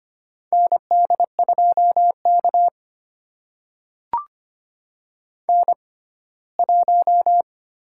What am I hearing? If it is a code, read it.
Morse code, heard as a keyed tone, about 25 wpm, spaced as for 17 wpm, 700 Hz. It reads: ND2K N1
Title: USA Call Signs - 4 Characters - 6x Character Spacing - 25wpm